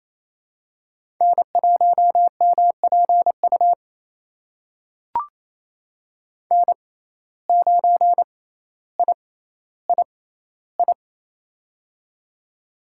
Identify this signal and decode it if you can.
Morse code, heard as a keyed tone, about 28 wpm, spaced as for 14 wpm, 700 Hz. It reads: N1MPU N9III
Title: USA Call Signs - 5 Characters - 6x Character Spacing - 28wpm